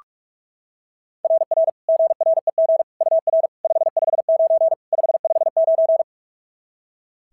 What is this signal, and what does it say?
Morse code, heard as a keyed tone, about 44 wpm, 690 Hz. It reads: RR GREG UR 559 559
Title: SOTA QSO Practice 45wpm